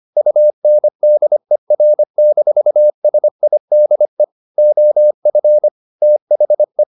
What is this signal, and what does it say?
Morse code, heard as a keyed tone, about 25 wpm, 600 Hz. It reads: UNDER-SIDE OF THE